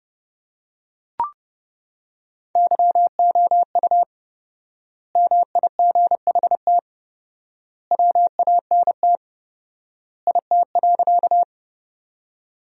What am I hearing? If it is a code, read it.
Morse code, heard as a keyed tone, about 30 wpm, 700 Hz. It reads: YOU MIGHT WANT IT.
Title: 4x Word-Spacing - Sentences from Top 300 Words - 30wpm